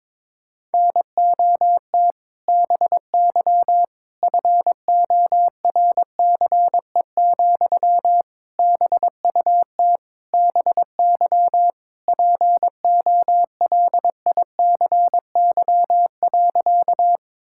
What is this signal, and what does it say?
Morse code, heard as a keyed tone, about 22 wpm, 700 Hz. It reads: NOT BY FORCE, BUT BY POLICY.
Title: Sentences from Top 1000 Words - Review Part 5 - 22wpm